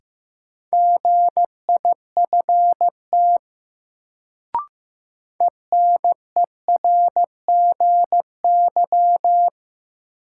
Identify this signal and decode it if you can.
Morse code, heard as a keyed tone, about 15 wpm, 700 Hz. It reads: GIFT ENERGY
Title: Sentences from Top 900 Words - New Word Drill - 15wpm